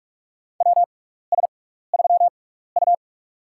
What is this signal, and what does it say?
Morse code, heard as a keyed tone, about 46 wpm, spaced as for 16 wpm, 700 Hz. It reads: WS3U
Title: USA Call Signs - 4 Characters - 6x Character Spacing - 45wpm